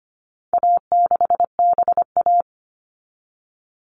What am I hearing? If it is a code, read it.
Morse code, heard as a keyed tone, about 25 wpm, 700 Hz. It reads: A6BA
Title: All Cut Numbers - 4-Digit 25wpm